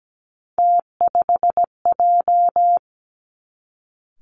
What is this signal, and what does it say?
Morse code, heard as a keyed tone, about 17 wpm, 700 Hz. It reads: T5J
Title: Three Challenging Letter-Number Combinations 17wpm